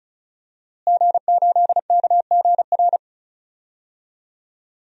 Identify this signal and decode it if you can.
Morse code, heard as a keyed tone, about 34 wpm, 700 Hz. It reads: G8KGR
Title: DXCC-100 Call Signs - 5 Characters - Repeated 2x - 35wpm